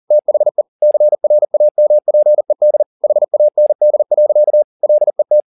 Morse code, heard as a keyed tone, about 40 wpm, 600 Hz. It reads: THE CRAMPED HAND. LET